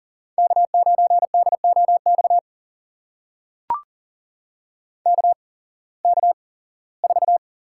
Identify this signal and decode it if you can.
Morse code, heard as a keyed tone, about 38 wpm, spaced as for 21 wpm, 700 Hz. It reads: K9DOX KK4
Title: USA Call Signs - 5 Characters - 8x Character Spacing - 40wpm